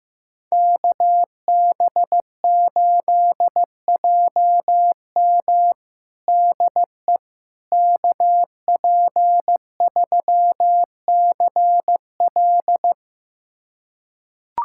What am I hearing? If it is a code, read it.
Morse code, heard as a keyed tone, about 15 wpm, 700 Hz. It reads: KB8JM DE KP3CL
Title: USA Callsign DE Callsign - 5 Characters - 15wpm